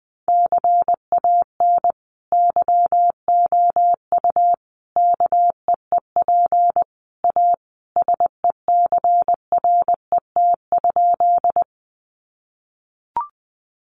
Morse code, heard as a keyed tone, about 20 wpm, 700 Hz. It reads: CAN YOU KEEP A SECRET?
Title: Sentences from Top 1200 Words 20wpm